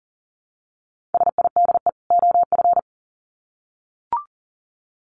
Morse code, heard as a keyed tone, about 40 wpm, 700 Hz. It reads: SIDE OF
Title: Sets of 2 Words 40wpm